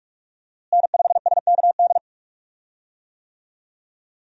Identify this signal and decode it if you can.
Morse code, heard as a keyed tone, about 48 wpm, 700 Hz. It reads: N5SKD
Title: USA Call Signs - 5 Characters - Repeated 2x - 45wpm